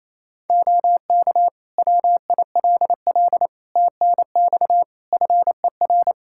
Morse code, heard as a keyed tone, about 27 wpm, 700 Hz. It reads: OK WILL TNX FER
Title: Practice QSO - Exchange 4 - Ending - 28wpm